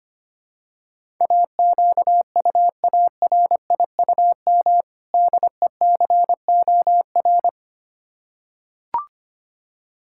Morse code, heard as a keyed tone, about 25 wpm, 700 Hz. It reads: AQUARIUM DECOR
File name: practice / 25wpm / Difficult Words with Hints - 25wpm